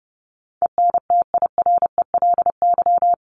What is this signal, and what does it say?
Morse code, heard as a keyed tone, about 30 wpm, 700 Hz. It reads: ENTIRELY
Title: LY Suffix - Top 100 Words with Base Words 30wpm